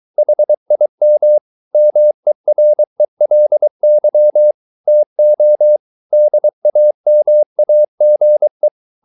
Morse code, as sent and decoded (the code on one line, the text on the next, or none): .... .. -- / -- . .-. .-.. -.-- / - --- / -.. .- -- .- --. .
HIM MERELY TO DAMAGE